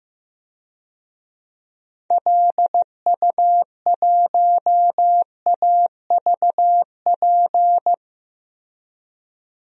Morse code, heard as a keyed tone, about 15 wpm, 700 Hz. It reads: LU1AVP
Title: DXCC-100 Call Signs - 6 Characters - Repeated 3x - 15wpm